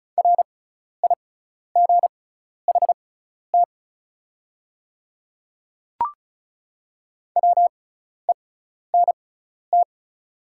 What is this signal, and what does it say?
Morse code, heard as a keyed tone, about 34 wpm, spaced as for 12 wpm, 690 Hz. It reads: RIGHT WENT
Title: 6x Character Spacing - Top 200 Words - 35wpm